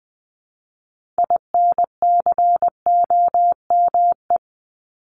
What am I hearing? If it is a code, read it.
Morse code, heard as a keyed tone, about 20 wpm, 700 Hz. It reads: INCOME
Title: Top 751-1000 Words 20wpm